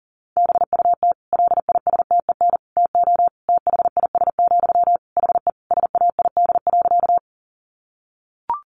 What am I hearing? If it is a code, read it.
Morse code, heard as a keyed tone, about 40 wpm, 700 Hz. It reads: BUT LISTEN TO THIS, HE SAID.
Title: Sentences from Top 400 Words 40wpm